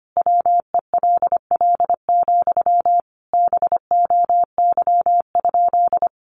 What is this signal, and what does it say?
Morse code, heard as a keyed tone, about 25 wpm, 700 Hz. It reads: WELL, BOY?
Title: Sentences from Top 1000 Words 25wpm